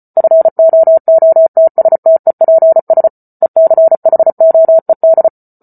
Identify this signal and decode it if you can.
Morse code, heard as a keyed tone, about 34 wpm, 650 Hz. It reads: FOOTSTEPS ECHOED